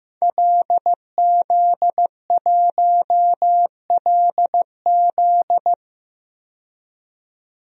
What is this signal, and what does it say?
Morse code, heard as a keyed tone, about 15 wpm, 700 Hz. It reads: LZ1LZ
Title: DXCC-100 Call Signs - 5 Characters - 15wpm